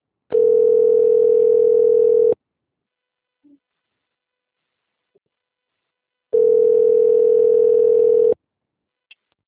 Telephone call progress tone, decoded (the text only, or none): ringback tone